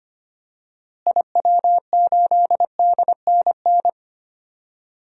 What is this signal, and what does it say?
Morse code, heard as a keyed tone, about 25 wpm, 700 Hz. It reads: IW8DNN
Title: DXCC-100 Call Signs - 6 Characters - Repeated 3x - 25wpm